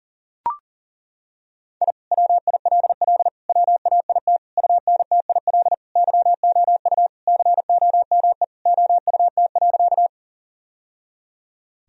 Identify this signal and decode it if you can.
Morse code, heard as a keyed tone, about 37 wpm, 700 Hz. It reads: I WILL WAIT UNTIL YOU COME OUT.
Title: Sentences from Top 1000 Words - Review Part 1 - 40wpm